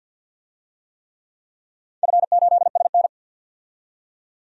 Morse code, heard as a keyed tone, about 47 wpm, 700 Hz. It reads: F8SN
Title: DXCC-100 Call Signs - 4 Characters - 50wpm